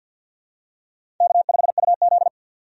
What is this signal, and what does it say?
Morse code, heard as a keyed tone, about 53 wpm, 700 Hz. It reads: K5UZ
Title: USA Call Signs - 4 Characters - 50wpm